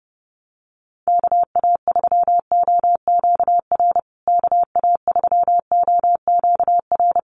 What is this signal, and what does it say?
Morse code, heard as a keyed tone, about 28 wpm, 700 Hz. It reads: KA3OQR KA3OQR
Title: USA Call Signs Repeated 3x 30wpm